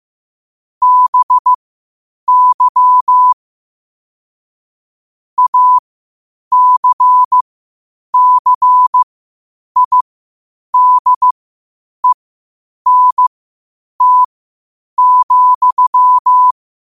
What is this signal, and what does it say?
Morse code, heard as a keyed tone, about 15 wpm, spaced as for 8 wpm, 1000 Hz. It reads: BY ACCIDENT,
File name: qotdfiles / 8wpm